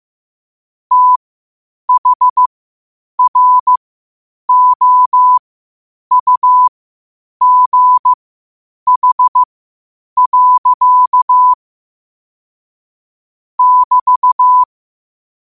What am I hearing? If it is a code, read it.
Morse code, heard as a keyed tone, about 15 wpm, spaced as for 8 wpm, 1000 Hz. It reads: THROUGH. <BT>
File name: qotdfiles / 8wpm